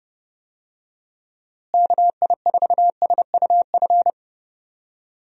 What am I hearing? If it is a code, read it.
Morse code, heard as a keyed tone, about 32 wpm, 700 Hz. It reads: KI4SUF
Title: USA Call Signs - 6 Characters - Repeated 2x - 30wpm